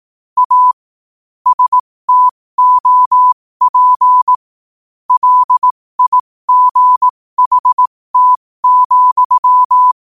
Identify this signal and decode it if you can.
Morse code, heard as a keyed tone, about 18 wpm, spaced as for 15 wpm, 1000 Hz. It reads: A STOP LIGHT,
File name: qotdfiles / 15wpm